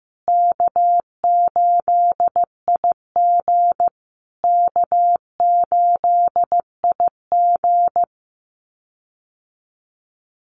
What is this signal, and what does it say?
Morse code, heard as a keyed tone, about 15 wpm, 700 Hz. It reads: K8IG K8IG
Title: USA Call Signs Repeated 3x 15wpm